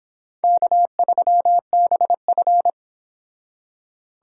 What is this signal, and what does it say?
Morse code, heard as a keyed tone, about 26 wpm, 700 Hz. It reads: K3BF